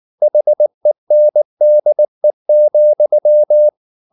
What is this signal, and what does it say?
Morse code, heard as a keyed tone, about 19 wpm, 600 Hz. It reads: HENDE,
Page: Ferie 19 wpm